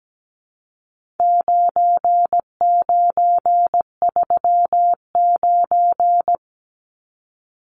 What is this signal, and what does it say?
Morse code, heard as a keyed tone, about 17 wpm, 700 Hz. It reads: 9939
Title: Numbers - 4-Digit 17wpm